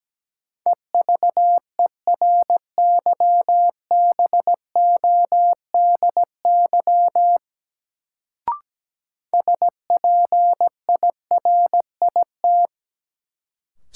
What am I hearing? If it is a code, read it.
Morse code, heard as a keyed tone, about 17 wpm, 700 Hz. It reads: EVERYBODY SPIRIT
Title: Sentences from Top 1400 Words - New Word Drill Alt - 17wpm